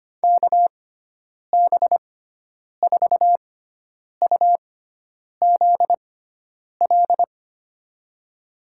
Morse code, heard as a keyed tone, about 25 wpm, spaced as for 9 wpm, 700 Hz. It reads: KB4UZL
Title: USA Call Signs - 6 Characters - 6x Character Spacing - 25wpm